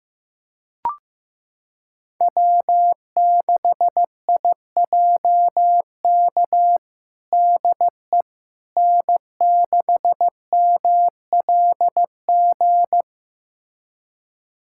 Morse code, heard as a keyed tone, about 15 wpm, 700 Hz. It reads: W6IJK DE N6MLG
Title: USA Callsign DE Callsign - 5 Characters - 15wpm